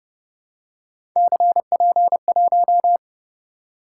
Morse code, heard as a keyed tone, about 31 wpm, 700 Hz. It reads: CP1